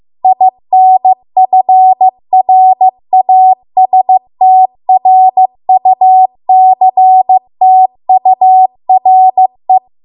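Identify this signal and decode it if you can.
Morse code, heard as a keyed tone, about 15 wpm, 750 Hz. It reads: INFRASTRUCTURE
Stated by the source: Morse code practice at 15 wpm from the national amateur radio association's code archive.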